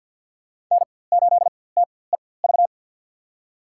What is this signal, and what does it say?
Morse code, heard as a keyed tone, about 48 wpm, spaced as for 23 wpm, 700 Hz. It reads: N8TEV
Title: USA Call Signs - 4x Character Spacing - 50wpm